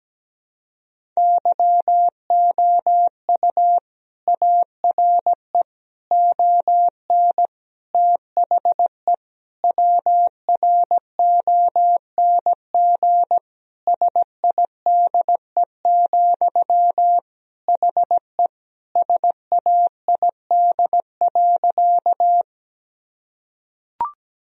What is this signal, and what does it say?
Morse code, heard as a keyed tone, about 17 wpm, 700 Hz. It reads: YOU ARE ON THE WRONG SIDE, HE SAID.
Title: Sentences from Top 1000 Words - Review Part 1 - 17wpm